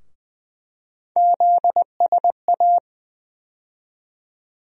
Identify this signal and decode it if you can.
Morse code, heard as a keyed tone, about 20 wpm, 700 Hz. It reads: ZSA
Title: USA Call Sign Postfixes - 3 Letters 20wpm